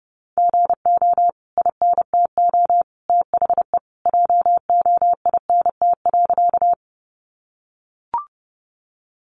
Morse code, heard as a keyed tone, about 30 wpm, 700 Hz. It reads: GO INTO THE JOINT.
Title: Sentences from Top 1300 Words 30wpm